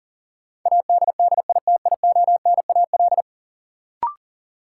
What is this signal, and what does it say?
Morse code, heard as a keyed tone, about 40 wpm, 700 Hz. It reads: ADDITIONAL
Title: Sentences from Top 1400 Words - New Word Drill Alt - 40wpm